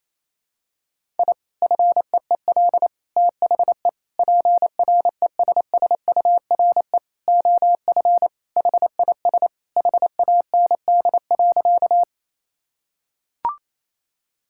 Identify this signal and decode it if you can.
Morse code, heard as a keyed tone, about 28 wpm, 700 Hz. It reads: I FEEL THE PRESSURE OF HIS HAND.
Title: Sentences from Top 1000 Words - Review Part 6 - 28wpm